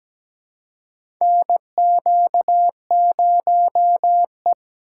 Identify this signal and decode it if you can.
Morse code, heard as a keyed tone, about 17 wpm, 700 Hz. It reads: NQ0E
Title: USA Call Signs - 4 Characters - 3x Character Spacing - 17wpm